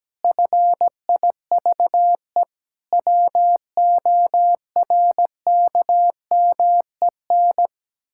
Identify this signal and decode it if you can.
Morse code, heard as a keyed tone, about 17 wpm, 700 Hz. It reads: FIVE WORKMEN
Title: Pangram Challenge v2 - 17wpm